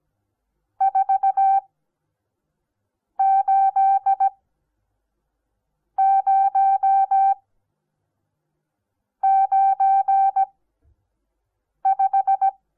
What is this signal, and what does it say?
Morse code, heard as a keyed tone, about 17 wpm, spaced as for 5 wpm, 780 Hz. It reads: 48095
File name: Onlinelehrgang / morsen / DJ4UF Morsen08